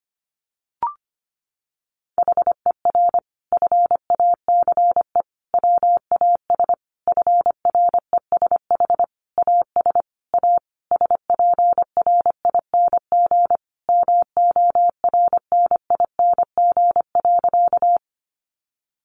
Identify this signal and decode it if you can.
Morse code, heard as a keyed tone, about 25 wpm, 700 Hz. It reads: HER FACE WAS FRESH AS A SPRING MORNING.
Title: Sentences from Top 1200 Words 25wpm